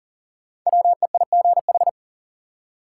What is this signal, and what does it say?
Morse code, heard as a keyed tone, about 40 wpm, 700 Hz. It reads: WEIGH
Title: Five Letter Words - 40wpm